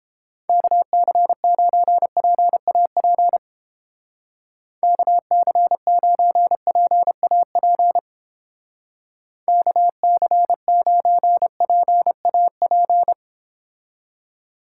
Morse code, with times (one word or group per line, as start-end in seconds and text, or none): 0.5-3.4 KC9PAP
4.8-8.0 KC9PAP
9.5-13.1 KC9PAP